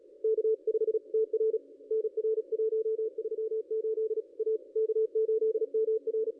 Morse code, heard as a keyed tone, about 35 wpm, 440 Hz. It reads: K5TR NR 138 A K8MR
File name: contests / ss-cw-k5tr-op-n5rz -0500